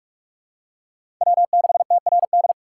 Morse code, heard as a keyed tone, about 41 wpm, 700 Hz. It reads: W6TRD
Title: USA Call Signs - 6x Character Spacing - 45wpm